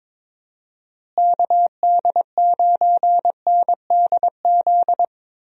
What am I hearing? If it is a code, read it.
Morse code, heard as a keyed tone, about 22 wpm, 700 Hz. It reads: KD9NDZ
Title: USA Call Signs - 6 Characters - Repeated 3x - 22wpm